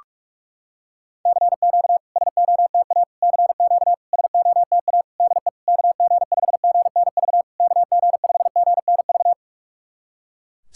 Morse code, heard as a keyed tone, about 47 wpm, 700 Hz. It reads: CQ SOTA CQ SOTA DE KG5GNV KG5GNV
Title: SOTA QSO Practice 45wpm